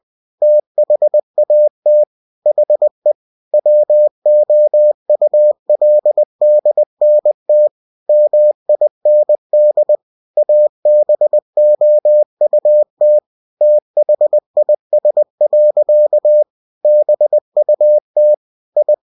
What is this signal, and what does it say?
Morse code, heard as a keyed tone, about 20 wpm, 600 Hz. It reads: THAT HE WOULDNT MIND ABOUT THIS. BUT I
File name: cw-books / winnie-the-pooh / winnie-20wpm-page-0092